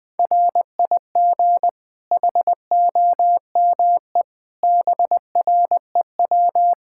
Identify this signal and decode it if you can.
Morse code, heard as a keyed tone, about 20 wpm, 700 Hz. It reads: RIG HOME BREW
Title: Practice QSO - Exchange 2 - Info - 20wpm